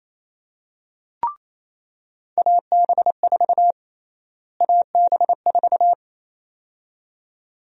Morse code, heard as a keyed tone, about 28 wpm, 700 Hz. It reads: AB4 AB4
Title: USA Call Sign Prefixes - Repeated 2x - 28wpm